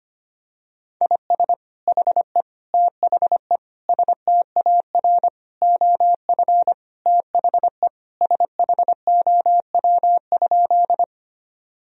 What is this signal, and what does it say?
Morse code, heard as a keyed tone, about 25 wpm, 700 Hz. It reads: IS HE THE STAR OF THE SHOW?
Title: Sentences from Top 1000 Words - Review Part 2 - 25wpm